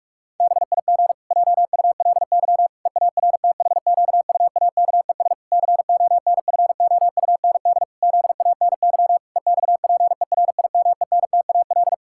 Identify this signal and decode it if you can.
Morse code, heard as a keyed tone, about 41 wpm, 700 Hz. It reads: BIG JULY EARTHQUAKES CONFOUND ZANY EXPERIMENTAL
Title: Pangram Challenge v2 - 45wpm